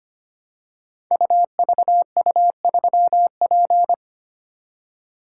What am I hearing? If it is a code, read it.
Morse code, heard as a keyed tone, about 25 wpm, 700 Hz. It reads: UVU3P
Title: Code Groups - 5 Letter:Numbers - 25wpm